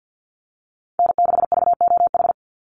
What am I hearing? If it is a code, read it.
Morse code, heard as a keyed tone, about 50 wpm, 700 Hz. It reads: N6VOH
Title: Call Signs - 50wpm